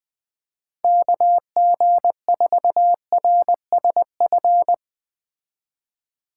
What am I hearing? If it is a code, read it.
Morse code, heard as a keyed tone, about 20 wpm, 700 Hz. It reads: KG4RSF